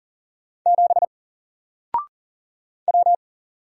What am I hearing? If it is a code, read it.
Morse code, heard as a keyed tone, about 39 wpm, 700 Hz. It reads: 7 W
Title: Single Letter-Number 40wpm